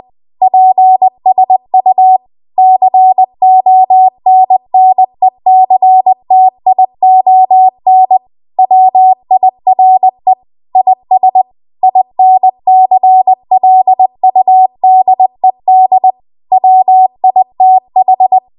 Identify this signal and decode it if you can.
Morse code, heard as a keyed tone, about 20 wpm, 750 Hz. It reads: PSU CONNECTION WIRE IS INCLUDED WITH
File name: Morse / 20WPM